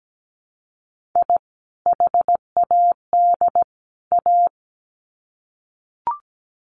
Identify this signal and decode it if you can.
Morse code, heard as a keyed tone, about 17 wpm, 700 Hz. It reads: I HAD A
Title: Sets of 3 Words 17wpm